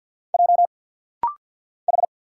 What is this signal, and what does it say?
Morse code, heard as a keyed tone, about 50 wpm, 700 Hz. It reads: J H
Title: Single Letter-Number - Rapid-Fire 50wpm